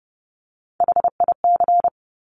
Morse code, heard as a keyed tone, about 30 wpm, 700 Hz. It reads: HIC
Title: Top Three Letter Combinations in Words 30wpm